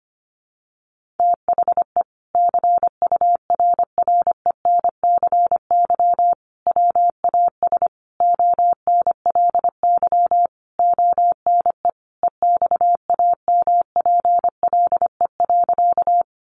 Morse code, heard as a keyed tone, about 25 wpm, 700 Hz. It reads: THE CURRENCY WAS ONLY ONE EXAMPLE.
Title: Sentences from Top 1500 Words 25wpm